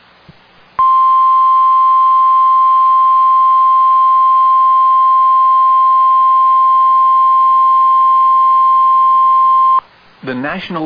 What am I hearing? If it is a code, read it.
Emergency Alert System (SAME) tones (the 1050 Hz attention tone), with speech. It is heard from 0.7 to 9.8 s.